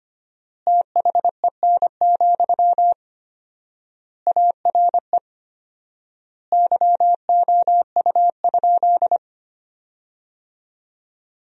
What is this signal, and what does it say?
Morse code, heard as a keyed tone, about 25 wpm, 700 Hz. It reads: THEN, ARE YOU?
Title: 4x Word-Spacing - Sentences from Top 100 Words - 25wpm